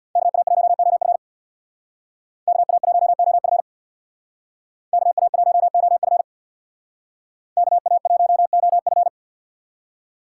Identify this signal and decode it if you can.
Morse code, heard as a keyed tone, about 61 wpm, 700 Hz. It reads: KA1OF KA1OF KA1OF KA1OF
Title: Speed Racing - USA 2x2 Call Signs 50wpm